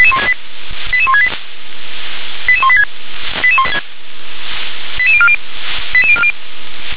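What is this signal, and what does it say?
Five-tone selective call (ZVEI), heard as quick runs of five tones, tones of about 70 ms. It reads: 88168 88186 88040 88040
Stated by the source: Five-tone ZVEI selective call received by SDR.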